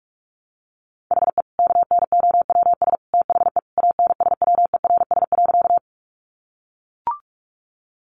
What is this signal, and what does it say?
Morse code, heard as a keyed tone, about 44 wpm, 700 Hz. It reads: HE KNOWS THE ANSWERS.
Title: Additional Easy Phrases 45wpm